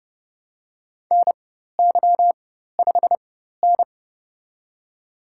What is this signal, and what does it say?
Morse code, heard as a keyed tone, about 30 wpm, spaced as for 14 wpm, 700 Hz. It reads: NY5N
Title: USA Call Signs - 4 Characters - 4x Character Spacing - 30wpm